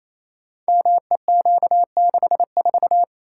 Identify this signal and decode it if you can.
Morse code, heard as a keyed tone, about 27 wpm, 700 Hz. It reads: MEQ64
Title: Code Groups - 5 Letter:Numbers - 28wpm